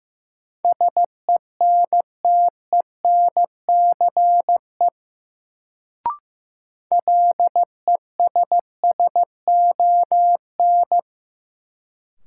Morse code, heard as a keyed tone, about 15 wpm, 700 Hz. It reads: SENTENCE LESSON
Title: Sentences from Top 1200 Words - New Word Drill Alt - 15wpm